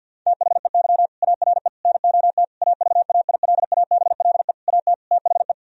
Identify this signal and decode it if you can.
Morse code, heard as a keyed tone, about 50 wpm, 700 Hz. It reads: THEY ARE NOT AVAILABLE AT THE
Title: Sentences from Top 1000 Words - Review Part 3 - 50wpm